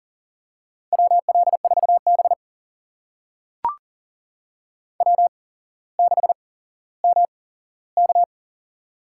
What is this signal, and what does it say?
Morse code, heard as a keyed tone, about 40 wpm, spaced as for 17 wpm, 700 Hz. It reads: WL4B W6MK
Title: USA Call Signs - 4 Characters - 8x Character Spacing - 40wpm